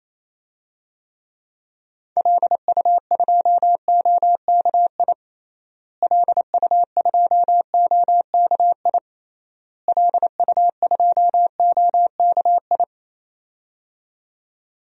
Morse code, heard as a keyed tone, about 28 wpm, 700 Hz. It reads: LU2OKI LU2OKI LU2OKI
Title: DXCC-100 Call Signs - 6 Characters - Repeated 3x - 28wpm